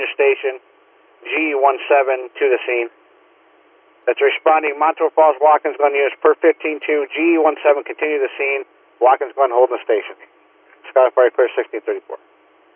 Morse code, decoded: none

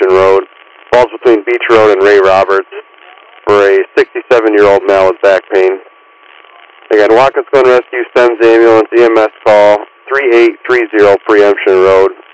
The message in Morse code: none

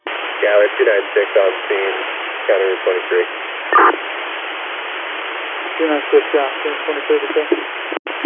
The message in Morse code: none

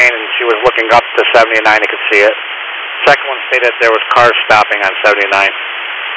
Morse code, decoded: none

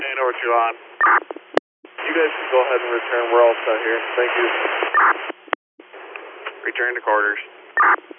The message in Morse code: none

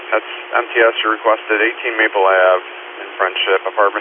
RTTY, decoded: none

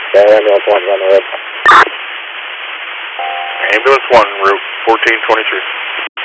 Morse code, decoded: none

